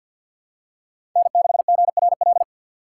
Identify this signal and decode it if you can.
Morse code, heard as a keyed tone, about 53 wpm, 690 Hz. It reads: N6GRL
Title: USA Call Signs - 5 Characters - 6x Character Spacing - 50wpm